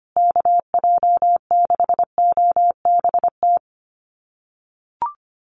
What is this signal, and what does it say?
Morse code, heard as a keyed tone, about 25 wpm, 700 Hz. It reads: KJ6OBT
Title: USA Call Signs Repeated 3x 25wpm